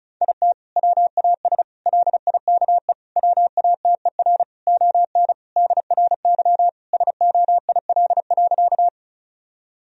Morse code, heard as a keyed tone, about 36 wpm, 700 Hz. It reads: IT WAS LIKE WATER ON DRY SOIL.
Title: Sentences from Top 1000 Words - Review Part 4 - 35wpm